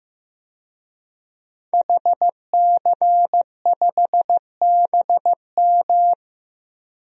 Morse code, heard as a keyed tone, about 15 wpm, 700 Hz. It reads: HC5BM
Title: DXCC-100 Call Signs - 5 Characters - 15wpm